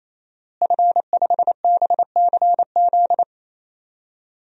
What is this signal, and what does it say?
Morse code, heard as a keyed tone, about 27 wpm, 700 Hz. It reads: F5BCZ
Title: DXCC-100 Call Signs - 5 Characters - Repeated 3x - 28wpm